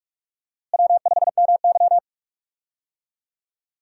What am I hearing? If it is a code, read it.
Morse code, heard as a keyed tone, about 42 wpm, 700 Hz. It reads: W5MY
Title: USA Call Signs - 4 Characters - Repeated 3x - 45wpm